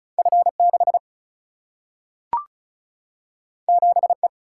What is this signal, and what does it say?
Morse code, heard as a keyed tone, about 36 wpm, 700 Hz. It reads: F 6 7 E
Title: Single-Letter Single-Number 35wpm